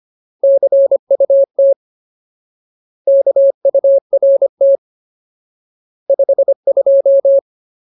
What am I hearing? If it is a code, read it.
Morse code, heard as a keyed tone, about 25 wpm, 550 Hz. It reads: CUT KURT 52
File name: Ultimate Word List / 25WPM Ultimate